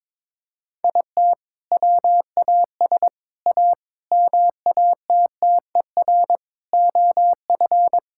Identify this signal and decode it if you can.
Morse code, heard as a keyed tone, about 22 wpm, 700 Hz. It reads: IT WAS A MATTER OF
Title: Sentences from Top 1000 Words - Review Part 5 - 22wpm